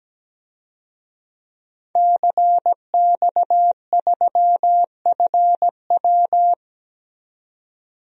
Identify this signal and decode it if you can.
Morse code, heard as a keyed tone, about 17 wpm, 700 Hz. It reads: CX3FW
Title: DXCC-100 Call Signs - 5 Characters - 17wpm